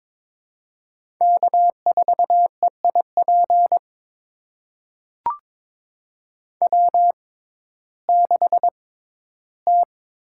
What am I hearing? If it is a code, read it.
Morse code, heard as a keyed tone, about 22 wpm, spaced as for 13 wpm, 700 Hz. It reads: K4EIP W6T
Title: USA Call Signs - 5 Characters - 6x Character Spacing - 22wpm